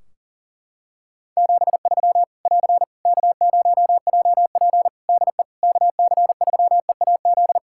Morse code, heard as a keyed tone, about 40 wpm, 700 Hz. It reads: 73 <AR> K0JP DE KC3EAZ
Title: Practice QSO - Exchange 4 - Ending - 40wpm